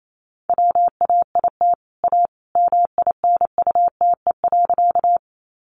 Morse code, heard as a keyed tone, about 28 wpm, 700 Hz. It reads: WAIT A MINUTE.
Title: Sentences from Top 500 Words 28wpm